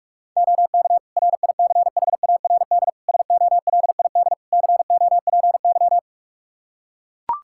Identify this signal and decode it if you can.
Morse code, heard as a keyed tone, about 41 wpm, 700 Hz. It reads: OK RICHARD SOLID COPY
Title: Practice QSO - Exchange 3 - More Info - 45wpm